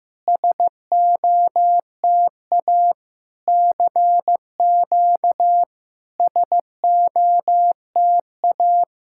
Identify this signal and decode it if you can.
Morse code, heard as a keyed tone, about 15 wpm, 700 Hz. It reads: SOTA CQ SOTA
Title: SOTA QSO Practice 15wpm